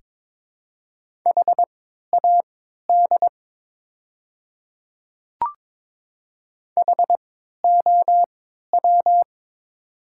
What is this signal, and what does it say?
Morse code, heard as a keyed tone, about 22 wpm, spaced as for 13 wpm, 700 Hz. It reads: HAD HOW
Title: 3x Character Spacing - Top 100 Words - 22wpm